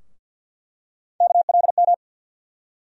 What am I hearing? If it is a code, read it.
Morse code, heard as a keyed tone, about 46 wpm, 700 Hz. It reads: KLM
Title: USA Call Sign Postfixes - 3 Letters 50wpm